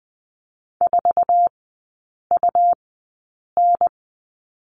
Morse code, heard as a keyed tone, about 20 wpm, spaced as for 8 wpm, 690 Hz. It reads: 4UN
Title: 159 Lesson 26e - Letter g - Three Characters 20wpm